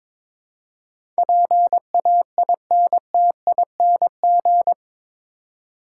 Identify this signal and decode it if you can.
Morse code, heard as a keyed tone, about 22 wpm, 700 Hz. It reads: PAINTING
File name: practice / 22wpm / Eight Letter Words - 22wpm